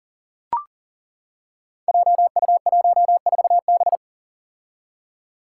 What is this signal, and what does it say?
Morse code, heard as a keyed tone, about 38 wpm, 700 Hz. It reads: JU14B
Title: Five Challenging Letter-Number Combinations 40wpm